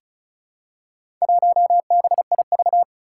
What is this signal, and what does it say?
Morse code, heard as a keyed tone, about 36 wpm, 700 Hz. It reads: 1BIV
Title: Four Challenging Letter-Number Combinations 35wpm